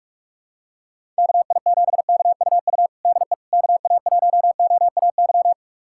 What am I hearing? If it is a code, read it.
Morse code, heard as a keyed tone, about 46 wpm, 700 Hz. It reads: KI7KUU DE KA1OAY
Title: USA Callsign DE Callsign 45wpm